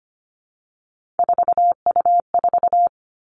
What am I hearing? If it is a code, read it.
Morse code, heard as a keyed tone, about 25 wpm, 700 Hz. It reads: 4U4
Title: Three Challenging Letter-Number Combinations 25wpm